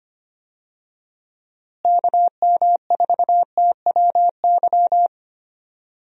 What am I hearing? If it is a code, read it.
Morse code, heard as a keyed tone, about 25 wpm, 700 Hz. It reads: KM4TWY
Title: USA Call Signs - 6 Characters - Repeated 2x - 25wpm